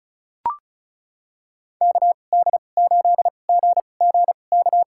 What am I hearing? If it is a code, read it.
Morse code, heard as a keyed tone, about 36 wpm, spaced as for 25 wpm, 700 Hz. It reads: KD8GGK
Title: USA Call Signs - 6 Characters - 2x Character Spacing - 35wpm